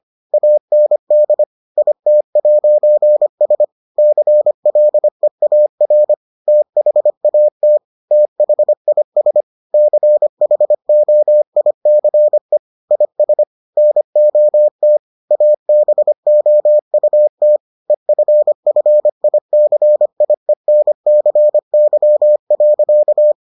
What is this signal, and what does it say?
Morse code, heard as a keyed tone, about 25 wpm, 600 Hz. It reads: AND IT'S CLEAR THAT THIS CHOICE IS NOT ABOUT EFFICIENCY.